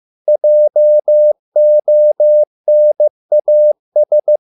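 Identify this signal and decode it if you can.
Morse code, heard as a keyed tone, about 15 wpm, 600 Hz. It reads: JONAS
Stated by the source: Morse code at a slow 15 wpm from CW med Gnister.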